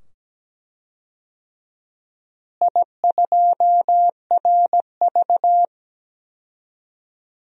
Morse code, heard as a keyed tone, about 17 wpm, 700 Hz. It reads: I2RV
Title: DXCC-100 Call Signs - 4 Characters - Repeated 2x - 17wpm